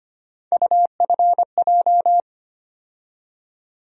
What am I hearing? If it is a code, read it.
Morse code, heard as a keyed tone, about 25 wpm, 700 Hz. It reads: UFJ
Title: USA Call Sign Postfixes - 3 Letters 25wpm